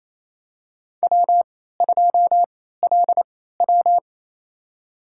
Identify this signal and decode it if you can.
Morse code, heard as a keyed tone, about 27 wpm, spaced as for 16 wpm, 700 Hz. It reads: W2LW
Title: USA Call Signs - 3x Character Spacing - 28wpm